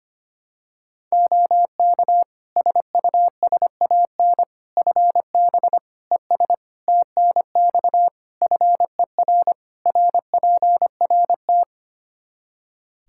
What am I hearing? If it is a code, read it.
Morse code, heard as a keyed tone, about 25 wpm, 700 Hz. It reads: OK SUSAN FB ES TNX FER RPRT
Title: Practice QSO - Exchange 2 - Info - 25wpm